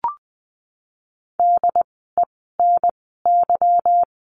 Morse code, heard as a keyed tone, about 20 wpm, spaced as for 15 wpm, 700 Hz. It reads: DENY
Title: 146 Lesson 24f - Letter y - Words 20wpm